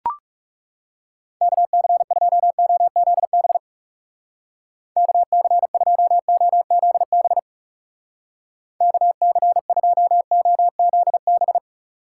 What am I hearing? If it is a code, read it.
Morse code, heard as a keyed tone, about 37 wpm, 700 Hz. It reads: KC2OZB KC2OZB KC2OZB
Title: Speed Racing - USA 2x3 Call Signs 30wpm